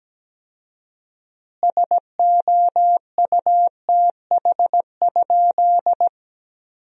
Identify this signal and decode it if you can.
Morse code, heard as a keyed tone, about 17 wpm, 700 Hz. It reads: SOUTH?
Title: 4x Word-Spacing - Sentences from Top 400 Words - 17wpm